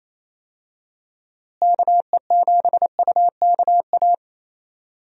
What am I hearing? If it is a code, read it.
Morse code, heard as a keyed tone, about 27 wpm, 700 Hz. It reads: KE7UKA
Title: USA Call Signs - 6 Characters - Repeated 2x - 28wpm